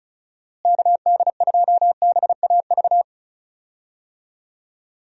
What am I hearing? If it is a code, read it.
Morse code, heard as a keyed tone, about 36 wpm, 700 Hz. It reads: KD2BAV
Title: USA Call Signs - 6 Characters - Repeated 3x - 35wpm